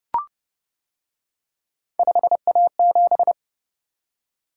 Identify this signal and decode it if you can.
Morse code, heard as a keyed tone, about 29 wpm, 700 Hz. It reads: 5A7